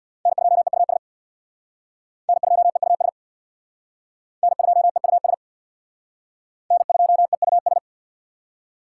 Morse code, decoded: N2EUS N2EUS N2EUS N2EUS

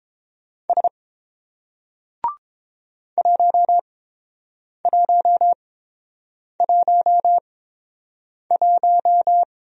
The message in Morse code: S 1111

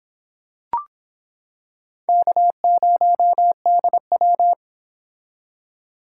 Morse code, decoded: K0DW

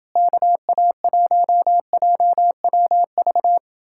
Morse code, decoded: KA1JWV

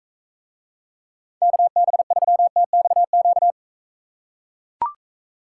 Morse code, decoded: KB3TXQ

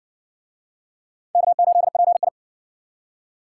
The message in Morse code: K8PI

